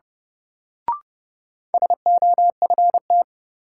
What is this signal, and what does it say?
Morse code, heard as a keyed tone, about 30 wpm, 700 Hz. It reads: SOFT
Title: Sentences from Top 1500 Words - New Word Drill - 30wpm